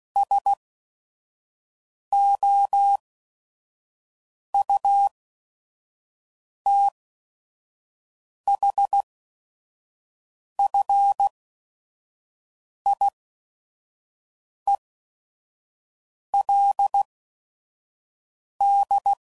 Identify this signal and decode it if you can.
Morse code, heard as a keyed tone, about 16 wpm, spaced as for 5 wpm, 800 Hz. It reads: SOUTHFIELD
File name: CW Code Groups @5WPM / fromSM's / Cities